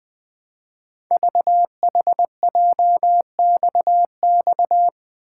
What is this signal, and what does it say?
Morse code, heard as a keyed tone, about 20 wpm, 700 Hz. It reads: VHJXX
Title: Code Groups - 5 Letters - 20wpm